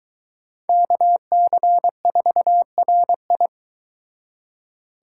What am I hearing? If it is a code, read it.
Morse code, heard as a keyed tone, about 23 wpm, 700 Hz. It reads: KC4RI